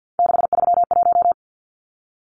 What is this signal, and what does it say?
Morse code, heard as a keyed tone, about 50 wpm, 700 Hz. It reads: 631
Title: Numbers - 3-Digit 50wpm